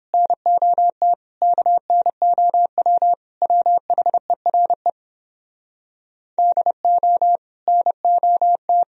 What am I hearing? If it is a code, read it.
Morse code, heard as a keyed tone, about 29 wpm, 700 Hz. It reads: NOT KNOW WHERE DO NOT